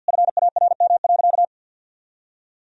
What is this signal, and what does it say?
Morse code, heard as a keyed tone, about 52 wpm, 700 Hz. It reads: FARM.
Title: Sentences from Top 1000 Words - Review Part 2 - 50wpm